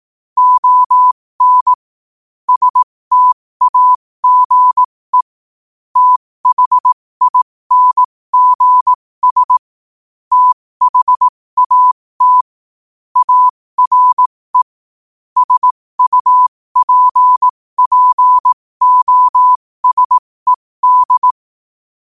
Morse code, heard as a keyed tone, about 18 wpm, spaced as for 15 wpm, 1000 Hz. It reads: ON STAGE THINGS THAT ARE SUPPOSED
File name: qotdfiles / 15wpm